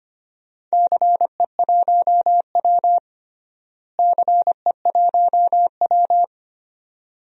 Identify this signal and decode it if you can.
Morse code, heard as a keyed tone, about 25 wpm, 700 Hz. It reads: CE1W CE1W
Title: DXCC-100 Call Signs - 4 Characters - Repeated 2x - 25wpm